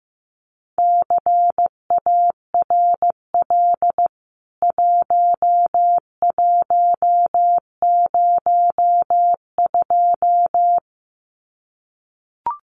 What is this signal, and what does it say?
Morse code, heard as a keyed tone, about 15 wpm, 700 Hz. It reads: CARL 1102
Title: CWT Exchange 15wpm